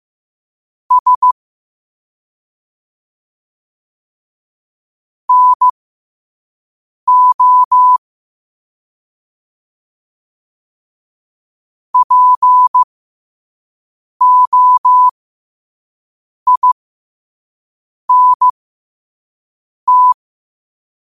Morse code, heard as a keyed tone, about 15 wpm, spaced as for 5 wpm, 1000 Hz. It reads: S NO POINT
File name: qotdfiles / 5wpm